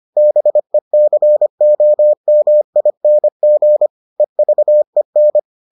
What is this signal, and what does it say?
Morse code, heard as a keyed tone, about 25 wpm, 600 Hz. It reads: BECOMING EVEN